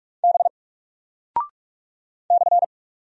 Morse code, heard as a keyed tone, about 44 wpm, 700 Hz. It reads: B /
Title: Single Character (Letter:Number:Punctuation:Prosign) Alt 45wpm